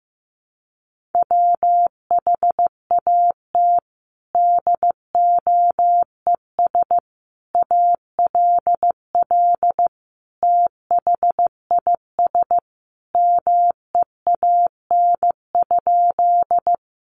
Morse code, heard as a keyed tone, about 15 wpm, 700 Hz. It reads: WHAT DOES ALL THIS MEAN?
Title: Sentences from Top 200 Words 15wpm